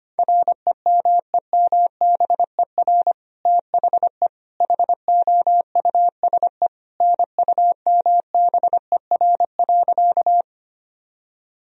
Morse code, heard as a keyed tone, about 25 wpm, 700 Hz. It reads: REMEMBER THE HOUSE NUMBER.
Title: Sentences from Top 1000 Words - Review Part 1 - 25wpm